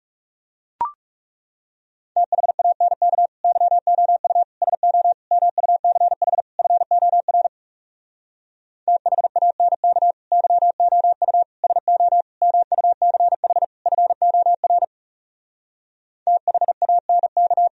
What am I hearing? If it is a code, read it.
Morse code, heard as a keyed tone, about 40 wpm, 700 Hz. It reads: THANK YOU SO MUCH FOR THANK YOU SO MUCH FOR THANK